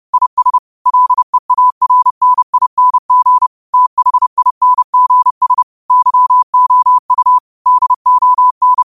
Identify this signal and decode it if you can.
Morse code, heard as a keyed tone, about 30 wpm, 1000 Hz. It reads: IS LEARNING THINGS YOU DON